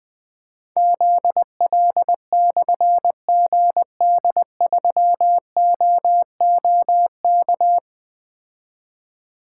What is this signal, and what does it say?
Morse code, heard as a keyed tone, about 21 wpm, 700 Hz. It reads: ZL/GD3OOK
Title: Difficult Call Signs 20wpm